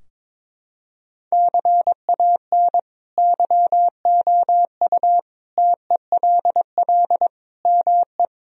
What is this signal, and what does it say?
Morse code, heard as a keyed tone, about 22 wpm, 700 Hz. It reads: CAN YOU TELL ME
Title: Sentences from Top 1000 Words - Review Part 3 - 22wpm